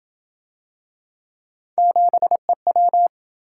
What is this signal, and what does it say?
Morse code, heard as a keyed tone, about 27 wpm, 700 Hz. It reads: 7EW